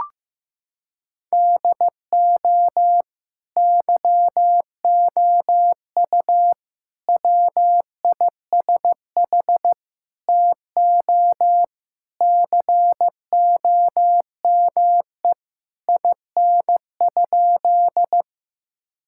Morse code, heard as a keyed tone, about 15 wpm, 700 Hz. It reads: DO YOU WISH TO COME IN?
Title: Sentences from Top 1600 Words 15wpm